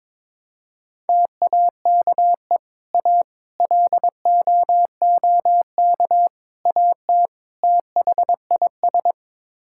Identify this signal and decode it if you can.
Morse code, heard as a keyed tone, about 22 wpm, 700 Hz. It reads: TAKE A LOOK AT THIS
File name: practice / 22wpm / Build A Phrase - 5-Word Phrases - 22wpm